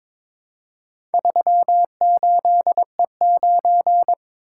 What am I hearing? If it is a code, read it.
Morse code, heard as a keyed tone, about 22 wpm, 700 Hz. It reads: 38E9
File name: practice / 22wpm / Four Letter-Number Combinations - 22wpm